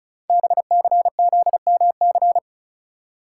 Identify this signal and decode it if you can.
Morse code, heard as a keyed tone, about 36 wpm, 690 Hz. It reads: BCZMC
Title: Code Groups - 5 Letters - 35wpm